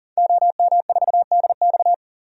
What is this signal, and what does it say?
Morse code, heard as a keyed tone, about 40 wpm, 700 Hz. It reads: OM4DX
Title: DXCC-100 Call Signs - 5 Characters - 40wpm